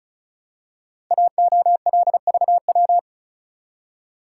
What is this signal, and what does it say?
Morse code, heard as a keyed tone, about 36 wpm, 700 Hz. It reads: AOLVW
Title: Code Groups - 5 Letters - 35wpm